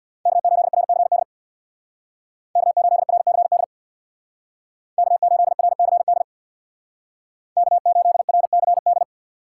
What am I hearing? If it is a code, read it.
Morse code, heard as a keyed tone, about 62 wpm, 700 Hz. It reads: K8RCD K8RCD K8RCD K8RCD
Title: Speed Racing - USA 1x3 Call Signs 50wpm